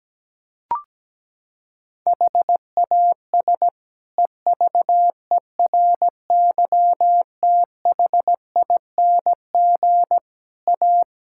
Morse code, heard as a keyed tone, about 17 wpm, 700 Hz. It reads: HAS EVERYTHING A
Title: Sentences from Top 1600 Words 17wpm